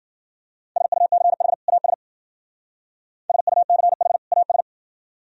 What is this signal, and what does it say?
Morse code, heard as a keyed tone, about 50 wpm, 700 Hz. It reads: SUCH AS SUCH AS